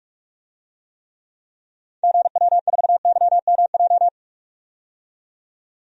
Morse code, heard as a keyed tone, about 47 wpm, 700 Hz. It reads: GW4YMJ
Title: DXCC-100 Call Signs - 6 Characters - 45wpm